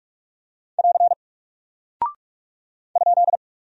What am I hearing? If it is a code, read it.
Morse code, heard as a keyed tone, about 46 wpm, 700 Hz. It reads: <AR> ?
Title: Single Character (Letter:Number:Punctuation:Prosign) Alt 45wpm